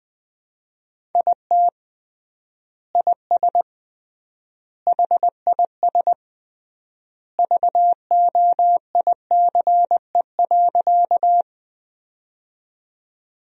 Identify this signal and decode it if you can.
Morse code, heard as a keyed tone, about 20 wpm, 700 Hz. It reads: IT IS HIS VOICE.
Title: 3x Word-Spacing - Sentences from Top 500 Words - 20wpm